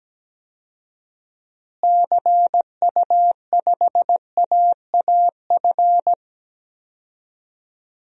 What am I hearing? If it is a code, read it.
Morse code, heard as a keyed tone, about 17 wpm, 700 Hz. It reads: CU5AAF
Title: DXCC-100 Call Signs - 6 Characters - Repeated 3x - 17wpm